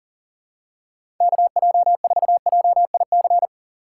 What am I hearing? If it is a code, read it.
Morse code, heard as a keyed tone, about 40 wpm, 700 Hz. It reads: KJ4JIC
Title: USA Call Signs - 6 Characters - 6x Character Spacing - 40wpm